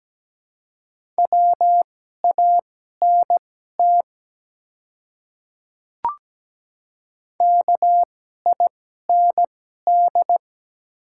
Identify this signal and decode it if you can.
Morse code, heard as a keyed tone, about 17 wpm, spaced as for 12 wpm, 700 Hz. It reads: WANT KIND
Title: 2x Character Spacing - Top 200 Words - 17wpm